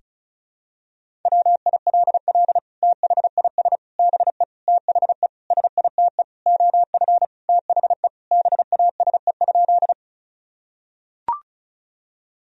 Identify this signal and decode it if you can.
Morse code, heard as a keyed tone, about 34 wpm, 700 Hz. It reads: WILL THIS BE THE SITE OF THE BASE?
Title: Sentences from Top 1000 Words - Review Part 1 - 35wpm